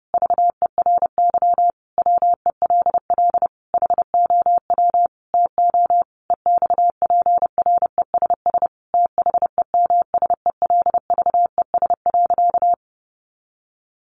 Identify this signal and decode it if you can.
Morse code, heard as a keyed tone, about 30 wpm, 700 Hz. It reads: VERY WELL HOW TO EXPRESS THEMSELVES.
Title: Sentences from Top 1100 Words 30wpm